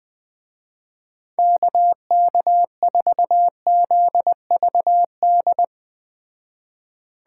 Morse code, heard as a keyed tone, about 20 wpm, 700 Hz. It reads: KK4ZVD
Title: Speed Racing - USA 2x3 Call Signs 17wpm